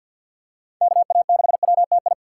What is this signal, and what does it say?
Morse code, heard as a keyed tone, about 52 wpm, 700 Hz. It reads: KA6WTI
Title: USA Call Signs - 2x Character Spacing - 50wpm